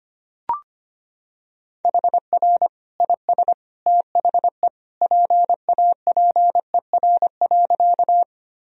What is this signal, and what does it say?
Morse code, heard as a keyed tone, about 25 wpm, 700 Hz. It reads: HR IS THE PAPER.
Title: Sentences from Top 300 Words and Abbreviations 25wpm